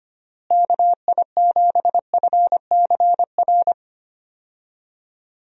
Morse code, heard as a keyed tone, about 25 wpm, 700 Hz. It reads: KI7FCR
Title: USA Call Signs - 6 Characters - Repeated 2x - 25wpm